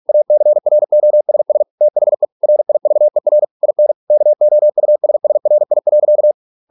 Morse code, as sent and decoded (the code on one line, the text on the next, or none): .- -.-. .-. --- ... ... / - .... / .-. .. ...- . .-. / .. -. / -.- --- ..- ... ... .-. .. .-.-.-
ACROSS THE RIVER IN KOUSSRI.